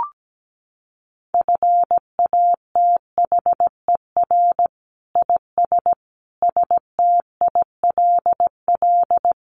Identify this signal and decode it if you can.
Morse code, heard as a keyed tone, about 17 wpm, 700 Hz. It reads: FATHER IS STILL
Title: Sentences from Top 1000 Words 17wpm